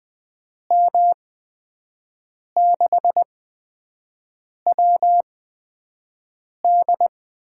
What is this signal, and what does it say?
Morse code, heard as a keyed tone, about 20 wpm, spaced as for 5 wpm, 700 Hz. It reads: M6WD
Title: USA Call Signs - 8x Character Spacing - 20wpm